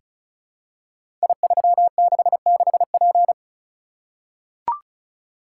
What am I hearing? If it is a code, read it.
Morse code, heard as a keyed tone, about 36 wpm, 700 Hz. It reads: I366P
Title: Code Groups - 5 Letter:Numbers - 35wpm